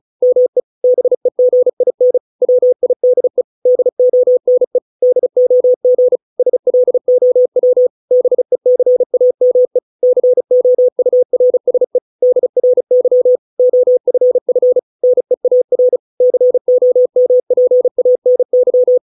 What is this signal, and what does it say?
Morse code, heard as a keyed tone, about 35 wpm, 500 Hz. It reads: ME BEGIN WIDE DONE DOG SLOW BECAME COURSE DRY OFF NEAR COMPANY